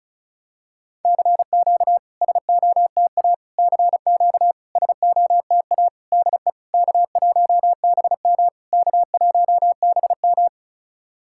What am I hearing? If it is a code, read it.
Morse code, heard as a keyed tone, about 34 wpm, 690 Hz. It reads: CQ SOTA CQ SOTA DE K1BM K1BM
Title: SOTA QSO Practice 35wpm